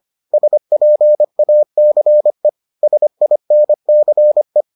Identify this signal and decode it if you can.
Morse code, heard as a keyed tone, about 25 wpm, 600 Hz. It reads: SPACE SINCE